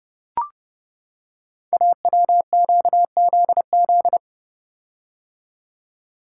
Morse code, heard as a keyed tone, about 30 wpm, 700 Hz. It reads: AWQZZ
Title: Code Groups - 5 Letters - 30wpm